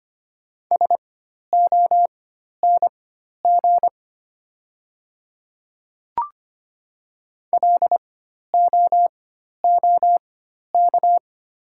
Morse code, heard as a keyed tone, about 25 wpm, spaced as for 12 wpm, 690 Hz. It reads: SONG LOOK
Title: 4x Character Spacing - Top 500 Words - 25wpm